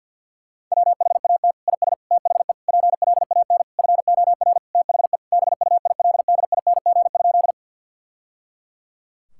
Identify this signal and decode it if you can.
Morse code, heard as a keyed tone, about 50 wpm, 700 Hz. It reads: WHAT IS THE PLAN FOR THE BUILDING?
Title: Sentences from Top 1000 Words - Review Part 5 - 50wpm